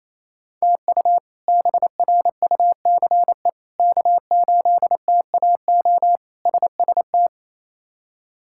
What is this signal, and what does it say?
Morse code, heard as a keyed tone, about 27 wpm, 700 Hz. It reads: TU BRUCE K8TAO SST
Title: K1USN SST QSO Practice - 28wpm